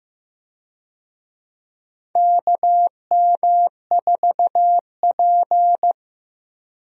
Morse code, heard as a keyed tone, about 15 wpm, 700 Hz. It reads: KM4P
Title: USA Call Signs - 4 Characters - Repeated 3x - 15wpm